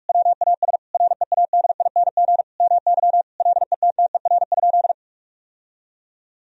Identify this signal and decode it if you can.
Morse code, heard as a keyed tone, about 46 wpm, 700 Hz. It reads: WAS READING MY LETTER?
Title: Sentences from Top 1000 Words - Review Part 3 - 45wpm